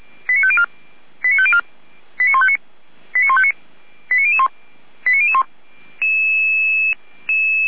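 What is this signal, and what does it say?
Five-tone selective call (ZVEI), heard as quick runs of five tones, tones of about 70 ms. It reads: 89504 89504 89159 89159 89001 89001
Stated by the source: ZVEI five-tone call received by SDR.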